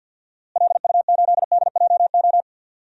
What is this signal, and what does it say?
Morse code, heard as a keyed tone, about 52 wpm, 700 Hz. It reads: LU8DJO
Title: DXCC-100 Call Signs - 6 Characters - Repeated 3x - 50wpm